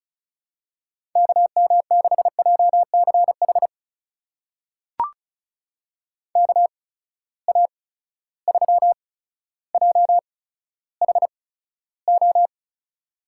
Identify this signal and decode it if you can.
Morse code, heard as a keyed tone, about 36 wpm, spaced as for 15 wpm, 700 Hz. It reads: KM6JCH KA3JHO
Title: USA Call Signs - 6 Characters - 8x Character Spacing - 35wpm